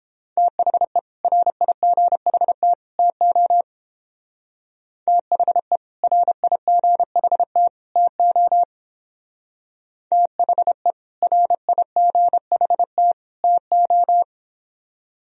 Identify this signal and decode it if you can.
Morse code, heard as a keyed tone, about 30 wpm, 700 Hz. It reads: THE RIGHT TO THE RIGHT TO THE RIGHT TO